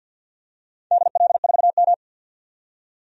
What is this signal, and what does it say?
Morse code, heard as a keyed tone, about 48 wpm, 700 Hz. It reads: DL4M
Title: DXCC-100 Call Signs - 4 Characters - 50wpm